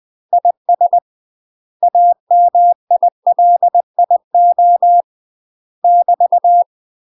Morse code, heard as a keyed tone, about 20 wpm, 700 Hz. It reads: IS AMILIO <BT>
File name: UK MASTER QSOs TRAINING/20WPM